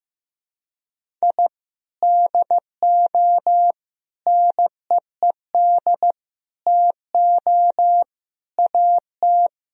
Morse code, heard as a keyed tone, about 15 wpm, 700 Hz. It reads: I DO NEED TO AT
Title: Sentences from Top 1000 Words - Review Part 1 - 15wpm